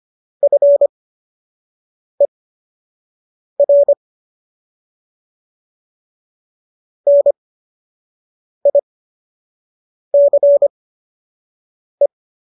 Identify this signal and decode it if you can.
Morse code, heard as a keyed tone, about 25 wpm, spaced as for 6 wpm, 570 Hz. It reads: FER NICE